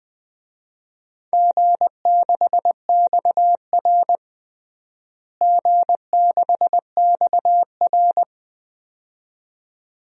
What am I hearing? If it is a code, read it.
Morse code, heard as a keyed tone, about 21 wpm, 700 Hz. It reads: G6XR G6XR
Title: DXCC-100 Call Signs - 4 Characters - Repeated 2x - 20wpm